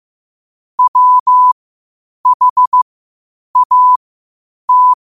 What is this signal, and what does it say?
Morse code, heard as a keyed tone, about 15 wpm, spaced as for 8 wpm, 1000 Hz. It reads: WHAT
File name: qotdfiles / 8wpm